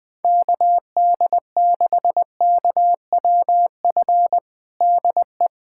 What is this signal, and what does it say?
Morse code, heard as a keyed tone, about 21 wpm, 700 Hz. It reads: KD6KWF DE
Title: USA Callsign DE Callsign - 6 Characters - 20wpm